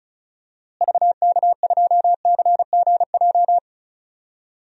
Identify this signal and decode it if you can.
Morse code, heard as a keyed tone, about 34 wpm, 700 Hz. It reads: VK2CGJ
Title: DXCC-100 Call Signs - 6 Characters - Repeated 3x - 35wpm